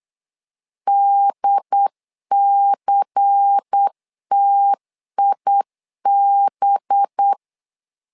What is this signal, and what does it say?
Morse code, heard as a keyed tone, about 8 wpm, 800 Hz. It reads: DCTIB
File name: LEZIONI / 5 LBVC / 5 LBVC 8wpm 2935